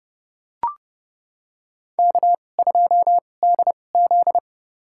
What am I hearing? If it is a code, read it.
Morse code, heard as a keyed tone, about 30 wpm, spaced as for 22 wpm, 700 Hz. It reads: K2DZ
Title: USA Call Signs - 2x Character Spacing - 30wpm